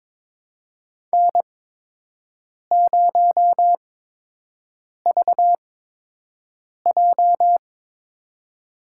Morse code, heard as a keyed tone, about 22 wpm, spaced as for 6 wpm, 700 Hz. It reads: N0VJ
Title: USA Call Signs - 8x Character Spacing - 22wpm